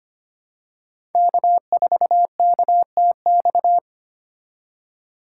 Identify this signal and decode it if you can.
Morse code, heard as a keyed tone, about 25 wpm, 700 Hz. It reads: K4KTX
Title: USA Call Signs - 4x Character Spacing - 25wpm